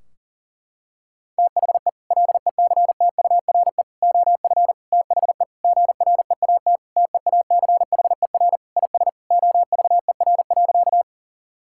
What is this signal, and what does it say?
Morse code, heard as a keyed tone, about 40 wpm, 700 Hz. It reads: THE LECTURE OF THE GREAT TEACHER IS OVER.
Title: Sentences from Top 1600 Words 40wpm